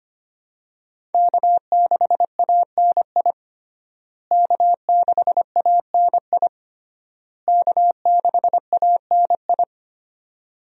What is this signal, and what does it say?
Morse code, heard as a keyed tone, about 25 wpm, 700 Hz. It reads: K6ANI K6ANI K6ANI
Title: USA Call Signs - 5 Characters - Repeated 3x - 25wpm